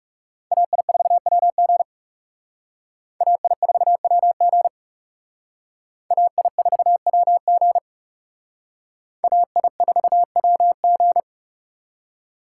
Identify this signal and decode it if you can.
Morse code, heard as a keyed tone, about 38 wpm, 700 Hz. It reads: AI4WG AI4WG AI4WG AI4WG